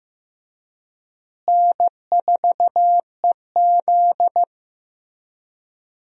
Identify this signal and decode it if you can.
Morse code, heard as a keyed tone, about 15 wpm, 700 Hz. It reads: N4EZ
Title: USA Call Signs - 4 Characters - Repeated 3x - 15wpm